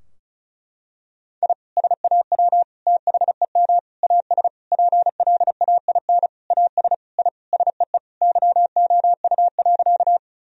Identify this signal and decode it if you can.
Morse code, heard as a keyed tone, about 36 wpm, 700 Hz. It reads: I SAW THEM AS PLAIN AS I SEE YOU.
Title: Sentences from Top 1000 Words - Review Part 2 - 35wpm